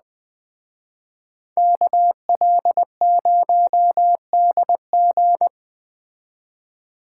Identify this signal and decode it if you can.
Morse code, heard as a keyed tone, about 20 wpm, 700 Hz. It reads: KL0DG
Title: Speed Racing - USA 2x2 Call Signs 17wpm